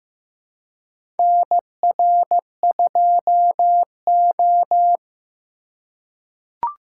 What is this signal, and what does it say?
Morse code, heard as a keyed tone, about 15 wpm, 700 Hz. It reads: NR2O
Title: USA Call Signs - 4 Characters - 2x Character Spacing - 15wpm